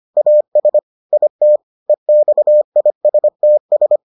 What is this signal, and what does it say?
Morse code, heard as a keyed tone, about 25 wpm, 600 Hz. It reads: AS IT EXISTS